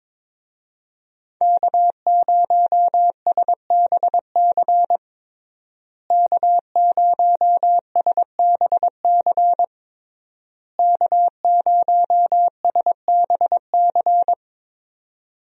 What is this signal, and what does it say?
Morse code, heard as a keyed tone, about 22 wpm, 700 Hz. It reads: K0SBC K0SBC K0SBC
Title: USA Call Signs - 5 Characters - Repeated 3x - 22wpm